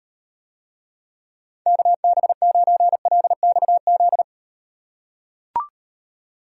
Morse code, heard as a keyed tone, about 38 wpm, 700 Hz. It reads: KB9LXZ